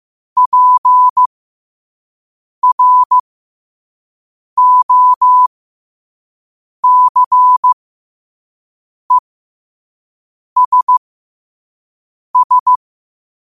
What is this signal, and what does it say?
Morse code, heard as a keyed tone, about 15 wpm, spaced as for 5 wpm, 1000 Hz. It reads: PROCESS
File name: qotdfiles / 5wpm